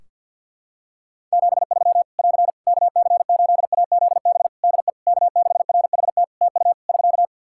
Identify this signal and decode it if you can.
Morse code, heard as a keyed tone, about 51 wpm, 700 Hz. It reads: 73 <AR> KC8AZB DE K6RHT TU <SK>
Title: Practice QSO - Exchange 4 - Ending - 50wpm